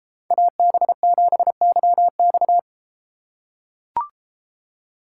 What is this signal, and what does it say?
Morse code, heard as a keyed tone, about 30 wpm, 700 Hz. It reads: AB7YX